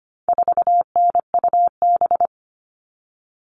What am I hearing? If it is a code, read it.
Morse code, heard as a keyed tone, about 25 wpm, 700 Hz. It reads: 4NUB
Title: All Cut Numbers - 4-Digit 25wpm